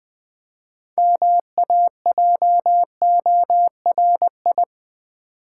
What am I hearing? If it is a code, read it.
Morse code, heard as a keyed tone, about 20 wpm, 700 Hz. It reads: MAJORI
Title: Build A Word - 3 Letters At A Time - 8-Letter Words - 20wpm